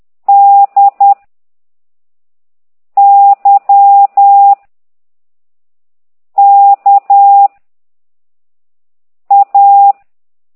Morse code, heard as a keyed tone, about 10 wpm, 800 Hz. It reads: D Y K A